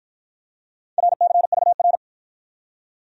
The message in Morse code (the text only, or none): RCVR